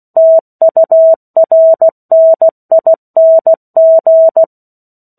Morse code, heard as a keyed tone, about 16 wpm, 650 Hz. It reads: TURNING
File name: 03-Chapter-03 16 WPM